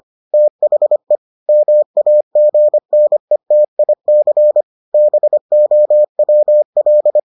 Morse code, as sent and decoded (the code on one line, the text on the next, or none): - .... / -- .- --. -. - .. -.-. / -... --- .-- .-..
THE MAGNETIC BOWL